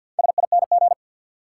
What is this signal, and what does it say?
Morse code, heard as a keyed tone, about 53 wpm, 700 Hz. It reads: SING
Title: Sentences from Top 400 Words - New Word Drill - 50wpm